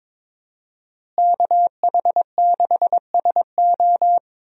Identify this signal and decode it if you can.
Morse code, heard as a keyed tone, about 21 wpm, 700 Hz. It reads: KH6SO
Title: USA Call Signs - 5 Characters - Repeated 3x - 22wpm